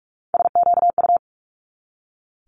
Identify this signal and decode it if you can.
Morse code, heard as a keyed tone, about 46 wpm, 700 Hz. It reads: SQU
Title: Multi-letter Phonemes 45wpm